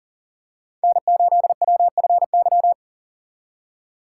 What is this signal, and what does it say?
Morse code, heard as a keyed tone, about 39 wpm, 700 Hz. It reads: N8WFY
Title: USA Call Signs - 5 Characters - 3x Character Spacing - 40wpm